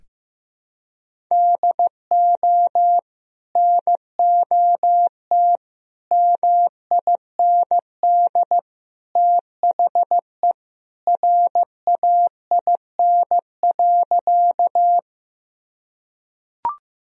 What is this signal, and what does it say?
Morse code, heard as a keyed tone, about 16 wpm, 700 Hz. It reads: DO NOT MIND THE RAIN.
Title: Sentences from Top 1000 Words - Review Part 1 - 15wpm